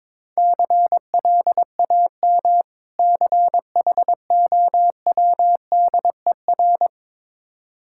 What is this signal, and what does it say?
Morse code, heard as a keyed tone, about 22 wpm, 690 Hz. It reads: CLAM CHOWDER
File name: Difficult Words with Hints - 22wpm